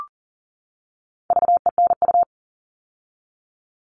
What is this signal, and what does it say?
Morse code, heard as a keyed tone, about 40 wpm, 700 Hz. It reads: VENU
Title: All Cut Numbers - 4-Digit 40wpm